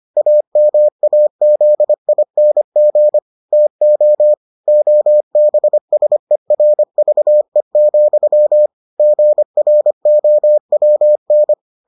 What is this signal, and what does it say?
Morse code, heard as a keyed tone, about 25 wpm, 600 Hz. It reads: AMAZING TO OBSERVE, GROWN